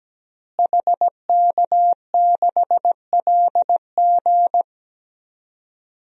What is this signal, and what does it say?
Morse code, heard as a keyed tone, about 17 wpm, 700 Hz. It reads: HK6LG
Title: DXCC-100 Call Signs - 5 Characters - Repeated 3x - 17wpm